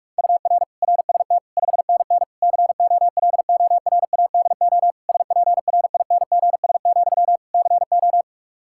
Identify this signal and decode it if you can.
Morse code, heard as a keyed tone, about 41 wpm, 700 Hz. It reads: UR RST 5NN COLORADO SPRINGS, CO
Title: Practice QSO - Exchange 1 - Beginning - 45wpm